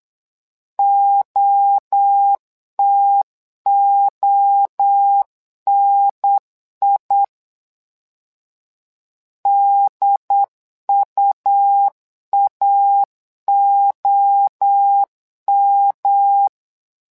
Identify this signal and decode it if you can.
Morse code, heard as a keyed tone, about 8 wpm, 800 Hz. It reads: OTONI DUAOM
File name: LEZIONI / 3 MWRO / 3 MWRO 8wpm 2606